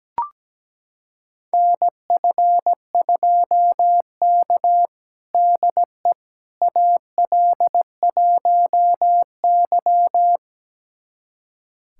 Morse code, heard as a keyed tone, about 17 wpm, 700 Hz. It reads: NF2K DE AL1Y
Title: USA Callsign DE Callsign - 4 Characters - 17wpm